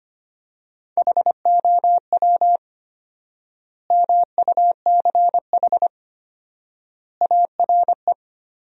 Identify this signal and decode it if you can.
Morse code, heard as a keyed tone, about 25 wpm, 700 Hz. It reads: HOW MUCH ARE
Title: 4x Word-Spacing - Sentences from Top 200 Words - 25wpm